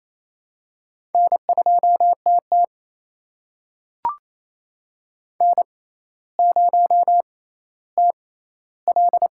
Morse code, heard as a keyed tone, about 29 wpm, spaced as for 15 wpm, 700 Hz. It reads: N2TT N0TL
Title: USA Call Signs - 4 Characters - 6x Character Spacing - 28wpm